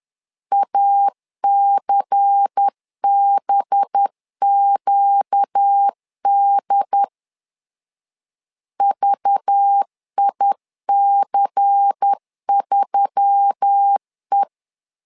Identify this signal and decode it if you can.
Morse code, heard as a keyed tone, about 10 wpm, 800 Hz. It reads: ACBQD VIC3E
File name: LEZIONI / 5 LBVC / 5 LBVC 10wpm 2349